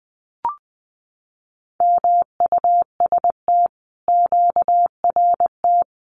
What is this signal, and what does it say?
Morse code, heard as a keyed tone, about 20 wpm, 700 Hz. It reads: MUST QRT
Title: Common QSO Phrases Encore 20wpm